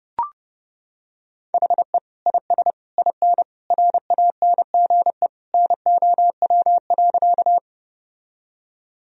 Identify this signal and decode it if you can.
Morse code, heard as a keyed tone, about 30 wpm, 700 Hz. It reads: HE IS IN RANGE NOW.
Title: Sentences from Top 1000 Words - Review Part 4 - 30wpm